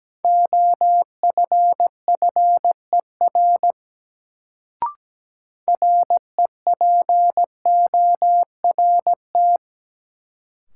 Morse code, heard as a keyed tone, about 17 wpm, 700 Hz. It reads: OFFER REPORT
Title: Sentences from Top 900 Words - New Word Drill - 17wpm